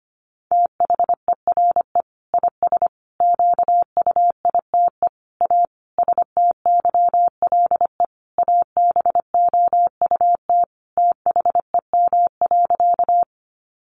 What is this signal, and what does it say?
Morse code, heard as a keyed tone, about 25 wpm, 700 Hz. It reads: THERE IS QUITE A STYLE ABOUT THEM.
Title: Sentences from Top 800 Words 25wpm